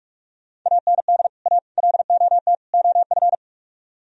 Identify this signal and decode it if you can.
Morse code, heard as a keyed tone, about 54 wpm, 700 Hz. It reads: AND A LOT OF
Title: Speed-Racing - Sets of 4 Words 45wpm